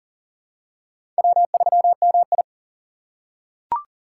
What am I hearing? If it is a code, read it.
Morse code, heard as a keyed tone, about 40 wpm, 700 Hz. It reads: W3MI
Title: USA Call Signs - 4 Characters - 6x Character Spacing - 40wpm